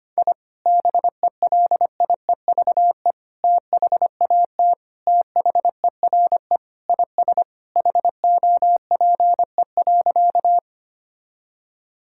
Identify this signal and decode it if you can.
Morse code, heard as a keyed tone, about 25 wpm, 700 Hz. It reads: I BELIEVE THAT THERE IS HOPE.
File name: practice / 25wpm / Sentences with 3-Word Phrases - 25wpm